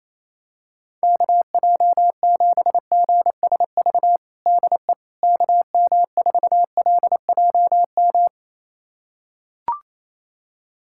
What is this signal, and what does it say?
Morse code, heard as a keyed tone, about 28 wpm, 700 Hz. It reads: KJ7GSV DE KM4LJM
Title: USA Callsign DE Callsign - 6 Characters - 28wpm